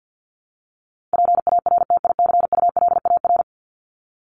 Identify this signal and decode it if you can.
Morse code, heard as a keyed tone, about 50 wpm, 700 Hz. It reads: PARTICULAR
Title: LY Suffix - Top 100 Words with Base Words 50wpm